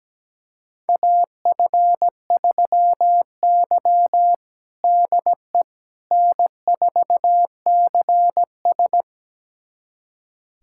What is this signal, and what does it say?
Morse code, heard as a keyed tone, about 17 wpm, 700 Hz. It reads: AF3Y DE N4CS
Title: USA Callsign DE Callsign - 4 Characters - 17wpm